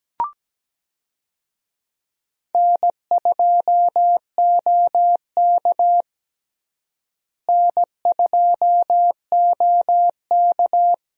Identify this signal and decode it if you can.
Morse code, heard as a keyed tone, about 17 wpm, 700 Hz. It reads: N2OK N2OK
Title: USA Call Signs - 4 Characters - Repeated 3x - 17wpm